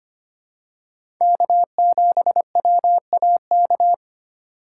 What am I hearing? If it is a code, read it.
Morse code, heard as a keyed tone, about 25 wpm, 700 Hz. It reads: K7WAK
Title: USA Call Signs - 3x Character Spacing - 25wpm